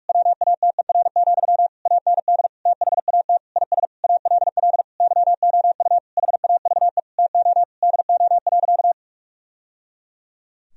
Morse code, heard as a keyed tone, about 46 wpm, 700 Hz. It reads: WATER, AND THAT IS ALL YOU HAVE TO DO.
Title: Sentences from Top 1000 Words - Review Part 2 - 45wpm